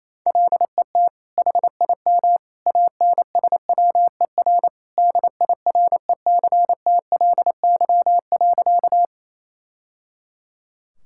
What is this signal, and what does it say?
Morse code, heard as a keyed tone, about 28 wpm, 700 Hz. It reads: LET HIM ANSWER DIRECTLY.
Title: Sentences from Top 1000 Words - Review Part 5 - 28wpm